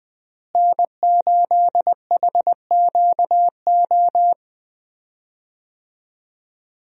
Morse code, heard as a keyed tone, about 20 wpm, 700 Hz. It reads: N8HQO
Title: USA Call Signs - 5 Characters - Repeated 3x - 20wpm